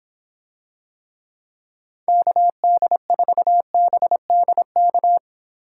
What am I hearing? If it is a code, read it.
Morse code, heard as a keyed tone, about 25 wpm, 700 Hz. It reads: KD4BDK